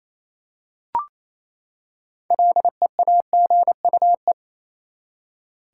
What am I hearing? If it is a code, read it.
Morse code, heard as a keyed tone, about 28 wpm, 700 Hz. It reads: LEAGUE
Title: Six Letter Words - 28wpm